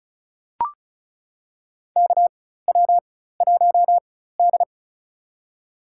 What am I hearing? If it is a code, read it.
Morse code, heard as a keyed tone, about 36 wpm, spaced as for 16 wpm, 700 Hz. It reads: KW1D
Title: USA Call Signs - 4 Characters - 4x Character Spacing - 35wpm